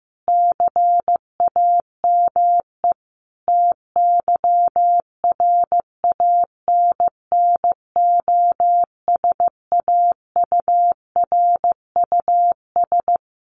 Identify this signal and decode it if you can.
Morse code, heard as a keyed tone, about 15 wpm, 700 Hz. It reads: CAME TYRANNOSAURUS
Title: A Sound Of Thunder 15wpm